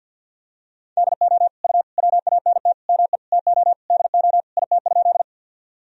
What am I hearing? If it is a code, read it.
Morse code, heard as a keyed tone, about 52 wpm, 700 Hz. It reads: DO U WANT ME TO DO IT?
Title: Sentences from Top 200 Words and Abbreviations 50wpm